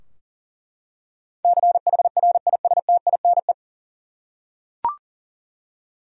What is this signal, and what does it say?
Morse code, heard as a keyed tone, about 40 wpm, 700 Hz. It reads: CHRISTINE
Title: USA Common Names - 40wpm